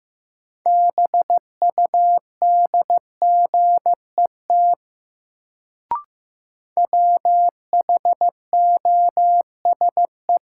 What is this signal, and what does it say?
Morse code, heard as a keyed tone, about 15 wpm, 700 Hz. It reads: BUDGET WHOSE
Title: Sentences from Top 1200 Words - New Word Drill - 15wpm